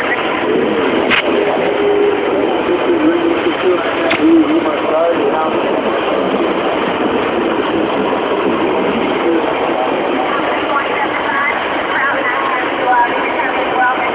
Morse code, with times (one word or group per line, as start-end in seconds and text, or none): none